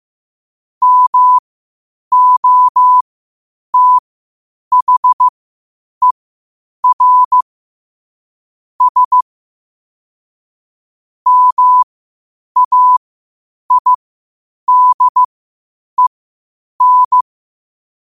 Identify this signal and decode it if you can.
Morse code, heard as a keyed tone, about 15 wpm, spaced as for 8 wpm, 1000 Hz. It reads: MOTHER S MAIDEN
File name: qotdfiles / 8wpm